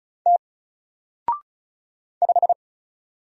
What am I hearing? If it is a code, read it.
Morse code, heard as a keyed tone, about 34 wpm, 700 Hz. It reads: T 5
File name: Single-Letter-Number-35wpm